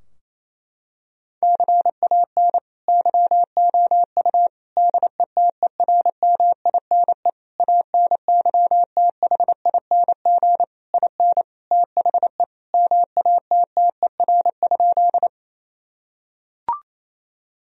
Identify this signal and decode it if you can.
Morse code, heard as a keyed tone, about 28 wpm, 700 Hz. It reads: CAN YOU DETERMINE ANYTHING IN THE MATTER?
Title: Sentences from Top 1000 Words - Review Part 4 - 28wpm